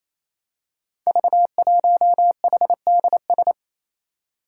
Morse code, heard as a keyed tone, about 27 wpm, 700 Hz. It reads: V1HDS
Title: Five Challenging Letter-Number Combinations 28wpm